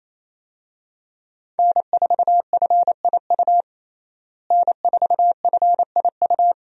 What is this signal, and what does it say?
Morse code, heard as a keyed tone, about 28 wpm, 700 Hz. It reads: N4FIU N4FIU
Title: USA Call Signs - 5 Characters - Repeated 2x - 28wpm